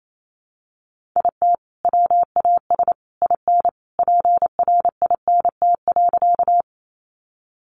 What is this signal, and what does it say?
Morse code, heard as a keyed tone, about 28 wpm, 700 Hz. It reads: IT WAS IN PRINT.
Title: Sentences from Top 1400 Words 28wpm